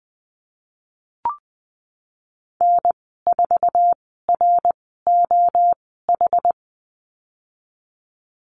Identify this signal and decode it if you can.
Morse code, heard as a keyed tone, about 19 wpm, spaced as for 14 wpm, 700 Hz. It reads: N4ROH
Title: 088 Lesson 15g - Number 5 - Call Signs 20wpm